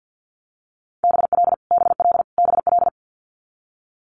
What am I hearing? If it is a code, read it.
Morse code, heard as a keyed tone, about 50 wpm, 700 Hz. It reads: BL BL BL
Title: Multi-letter Phonemes Repeated 3x 50wpm